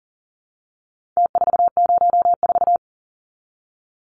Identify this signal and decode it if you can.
Morse code, heard as a keyed tone, about 40 wpm, 700 Hz. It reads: T404
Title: Numbers - Contest Style - 4-Digit 40wpm